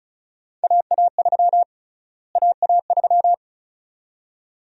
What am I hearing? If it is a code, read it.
Morse code, heard as a keyed tone, about 36 wpm, 700 Hz. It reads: AA3 AA3
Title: USA Call Sign Prefixes - Repeated 2x - 35wpm